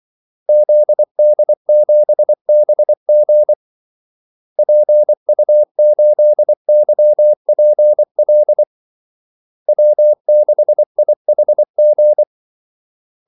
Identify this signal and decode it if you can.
Morse code, heard as a keyed tone, about 24 wpm, 600 Hz. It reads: ZD7BG PU8YPL W6IHG
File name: Callsigns-24wpm